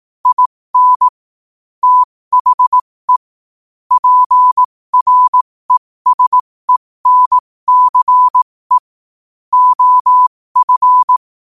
Morse code, heard as a keyed tone, about 18 wpm, spaced as for 15 wpm, 1000 Hz. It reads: IN THE PRESENCE OF
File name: qotdfiles / 15wpm